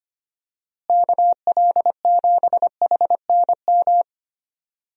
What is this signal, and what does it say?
Morse code, heard as a keyed tone, about 25 wpm, 700 Hz. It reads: KL7HNM
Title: USA Call Signs - 6 Characters - Repeated 3x - 25wpm